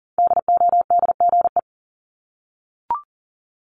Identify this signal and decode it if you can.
Morse code, heard as a keyed tone, about 40 wpm, 700 Hz. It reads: DODGE
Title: Multi-letter Phonemes Repeated 3x 40wpm